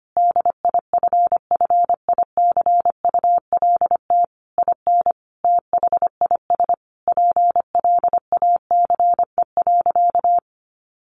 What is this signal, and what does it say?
Morse code, heard as a keyed tone, about 25 wpm, 700 Hz. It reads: DIFFICULT IN THIS PLACE.
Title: Sentences from Top 1500 Words 25wpm